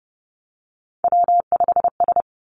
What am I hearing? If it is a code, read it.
Morse code, heard as a keyed tone, about 30 wpm, 700 Hz. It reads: W5S
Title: Three Challenging Letter-Number Combinations 30wpm